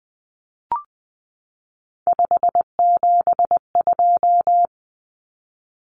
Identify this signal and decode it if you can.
Morse code, heard as a keyed tone, about 20 wpm, 700 Hz. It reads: 572
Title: Common Cut Numbers - 3-Digit 20wpm